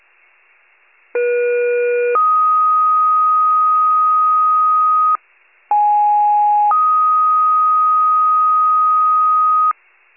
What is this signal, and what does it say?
Two-tone page, heard as a one-second tone, then a three-second tone, about 480 then 1250 Hz, about 830 then 1250 Hz. It is heard from 1.1 to 9.7 s.